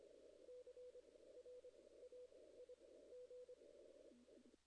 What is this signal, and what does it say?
Morse code, heard as a keyed tone, about 25 wpm, spaced as for 16 wpm, 500 Hz. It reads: CRAIG
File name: 2016 JAN CW NAQP / 2016 Jan naqp k5ot - 0-8